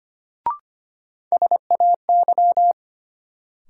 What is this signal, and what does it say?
Morse code, heard as a keyed tone, about 25 wpm, 700 Hz. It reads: SAY
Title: Sentences from Top 200 Words - New Word Drill - 25wpm